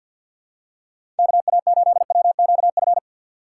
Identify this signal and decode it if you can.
Morse code, heard as a keyed tone, about 53 wpm, 700 Hz. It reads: KA8WQF
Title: USA Call Signs - 6 Characters - 3x Character Spacing - 50wpm